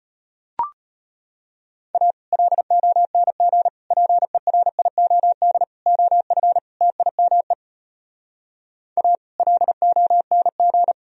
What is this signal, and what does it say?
Morse code, heard as a keyed tone, about 33 wpm, 700 Hz. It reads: A LONG PERIOD OF TIME A LONG